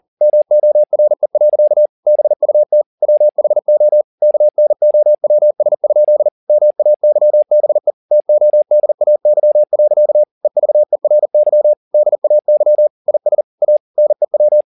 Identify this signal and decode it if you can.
Morse code, heard as a keyed tone, about 40 wpm, 600 Hz. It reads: MORE. BUT WHO KNOWS? MAYBE TODAY. EVERY DAY IS A NEW